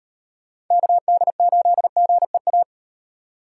Morse code, heard as a keyed tone, about 38 wpm, 700 Hz. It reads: KD8GEA